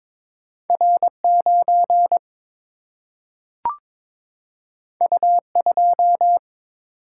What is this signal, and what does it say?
Morse code, heard as a keyed tone, about 22 wpm, 700 Hz. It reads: R9 U2
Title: Single-Letter Single-Number 22wpm